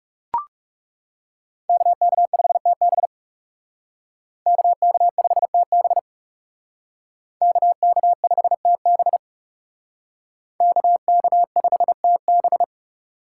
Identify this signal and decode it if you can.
Morse code, heard as a keyed tone, about 38 wpm, 700 Hz. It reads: KK5TB KK5TB KK5TB KK5TB